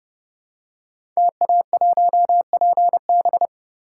Morse code, heard as a keyed tone, about 28 wpm, 700 Hz. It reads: TA1PB
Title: Code Groups - 5 Letter:Numbers - 30wpm